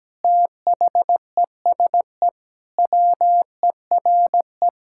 Morse code, heard as a keyed tone, about 17 wpm, 700 Hz. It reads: THESE WERE
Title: Sentences from Top 1000 Words - Review Part 5 - 17wpm